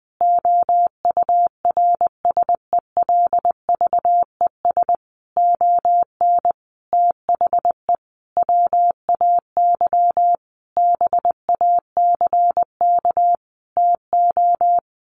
Morse code, heard as a keyed tone, about 20 wpm, 700 Hz. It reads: OURSELVES ON THE WAY BACK TO
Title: A Sound Of Thunder 20wpm